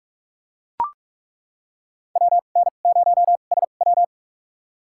Morse code, heard as a keyed tone, about 42 wpm, spaced as for 32 wpm, 700 Hz. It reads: WN0SW
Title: USA Call Signs - 5 Characters - 2x Character Spacing - 45wpm